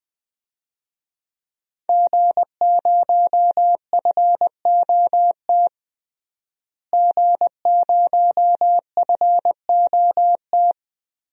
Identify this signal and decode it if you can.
Morse code, heard as a keyed tone, about 20 wpm, 700 Hz. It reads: G0FOT G0FOT
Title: DXCC-100 Call Signs - 5 Characters - Repeated 3x - 20wpm